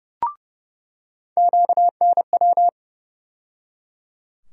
Morse code, heard as a keyed tone, about 30 wpm, 700 Hz. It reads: QNW
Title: USA Call Sign Postfixes - 3 Letters 30wpm